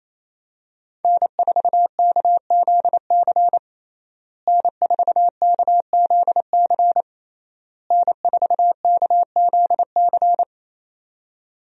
Morse code, heard as a keyed tone, about 28 wpm, 700 Hz. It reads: N4KZC N4KZC N4KZC
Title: USA Call Signs - 5 Characters - Repeated 3x - 28wpm